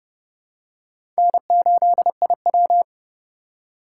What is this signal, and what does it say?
Morse code, heard as a keyed tone, about 31 wpm, 700 Hz. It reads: N8IW
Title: USA Call Signs - 4 Characters - 4x Character Spacing - 30wpm